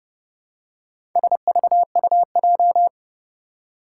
Morse code, heard as a keyed tone, about 30 wpm, 700 Hz. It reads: SVUJ
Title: Four Challenging Letter-Number Combinations 30wpm